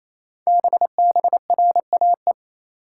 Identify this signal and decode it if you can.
Morse code, heard as a keyed tone, about 27 wpm, 700 Hz. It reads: BBRAE
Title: Code Groups - 5 Letters - 28wpm